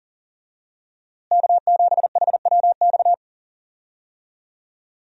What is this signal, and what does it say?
Morse code, heard as a keyed tone, about 40 wpm, 700 Hz. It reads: K7HWX
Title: USA Call Signs - 5 Characters - Repeated 3x - 40wpm